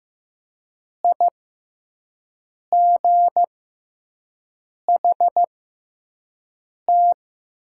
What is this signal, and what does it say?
Morse code, heard as a keyed tone, about 15 wpm, spaced as for 5 wpm, 700 Hz. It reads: IGHT
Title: 6x Character Spacing - Top 500 Words - 15wpm